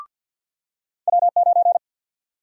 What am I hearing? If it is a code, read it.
Morse code, heard as a keyed tone, about 52 wpm, 700 Hz. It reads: W9
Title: USA Call Sign Prefixes - 50wpm